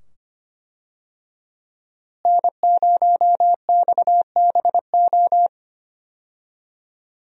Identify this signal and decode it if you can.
Morse code, heard as a keyed tone, about 25 wpm, 700 Hz. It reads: N0XBO
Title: USA Call Signs - 5 Characters - Repeated 2x - 25wpm